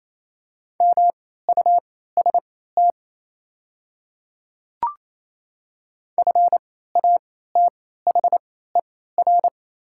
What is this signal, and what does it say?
Morse code, heard as a keyed tone, about 28 wpm, spaced as for 16 wpm, 700 Hz. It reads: MUST FATHER
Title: 3x Character Spacing - Top 300 Words - 28wpm